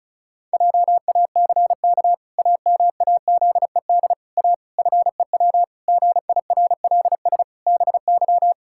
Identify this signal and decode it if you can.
Morse code, heard as a keyed tone, about 36 wpm, 700 Hz. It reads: JACK AMAZED A FEW GIRLS BY
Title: Pangram Challenge v2 - 35wpm